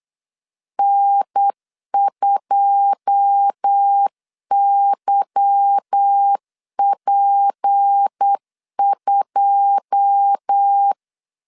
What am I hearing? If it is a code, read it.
Morse code, heard as a keyed tone, about 8 wpm, 800 Hz. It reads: N2YP2
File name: LEZIONI / 9 2 8wpm 3437